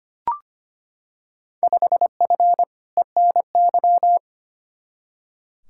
Morse code, heard as a keyed tone, about 25 wpm, 700 Hz. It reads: 5F ENY
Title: Field Day Exchanges Encore - 25wpm